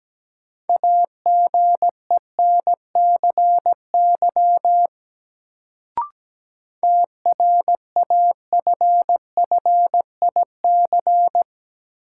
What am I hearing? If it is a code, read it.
Morse code, heard as a keyed tone, about 17 wpm, 700 Hz. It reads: AGENCY TRAFFIC
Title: Sentences from Top 1300 Words - New Word Drill - 17wpm